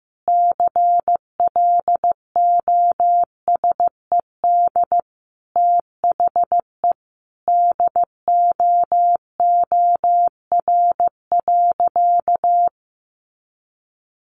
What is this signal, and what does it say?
Morse code, heard as a keyed tone, about 15 wpm, 700 Hz. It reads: CLOSED THE DOOR.
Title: Sentences from Top 1400 Words 15wpm